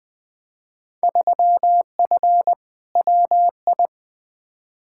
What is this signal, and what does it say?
Morse code, heard as a keyed tone, about 20 wpm, 700 Hz. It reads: 3F WI
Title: Field Day Exchanges - 20wpm